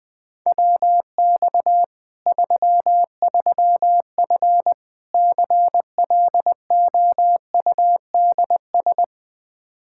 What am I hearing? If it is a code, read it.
Morse code, heard as a keyed tone, about 21 wpm, 700 Hz. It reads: WX 33F CLOUDS
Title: Practice QSO - Exchange 2 - Info - 20wpm